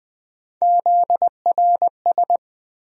Morse code, heard as a keyed tone, about 20 wpm, 700 Hz. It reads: ZRS
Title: USA Call Sign Postfixes - 3 Letters 20wpm